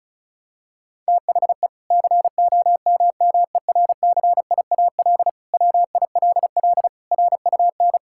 Morse code, heard as a keyed tone, about 36 wpm, 700 Hz. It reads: THE COMMERCIAL WILL RUN
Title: Sentences from Top 1000 Words - Review Part 5 - 35wpm